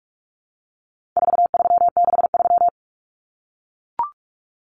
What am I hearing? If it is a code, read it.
Morse code, heard as a keyed tone, about 44 wpm, 700 Hz. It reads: 4363
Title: Common Cut Numbers - 4-Digit 45wpm